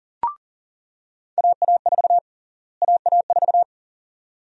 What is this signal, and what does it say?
Morse code, heard as a keyed tone, about 39 wpm, 700 Hz. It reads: AA4 AA4
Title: USA Call Sign Prefixes - Repeated 2x - 40wpm